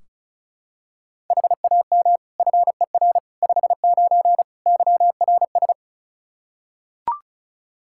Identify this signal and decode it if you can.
Morse code, heard as a keyed tone, about 36 wpm, 700 Hz. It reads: HAM FER 59 YRS
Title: Practice QSO - Exchange 3 - More Info - 35wpm